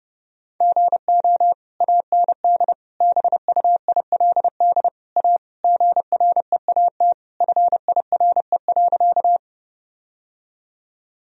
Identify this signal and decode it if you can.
Morse code, heard as a keyed tone, about 32 wpm, 700 Hz. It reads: GO AND BUILD A GREAT FIRE.
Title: Sentences from Top 1000 Words - Review Part 1 - 30wpm